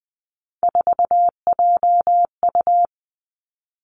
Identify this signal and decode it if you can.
Morse code, heard as a keyed tone, about 20 wpm, 700 Hz. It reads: 4JU
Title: Three Challenging Letter-Number Combinations 20wpm